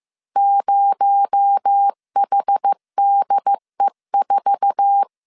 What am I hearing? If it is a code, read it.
Morse code, heard as a keyed tone, about 15 wpm, 800 Hz. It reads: 0HDE4
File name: CW / DET / 1 9 1500C S C 14wpm 3245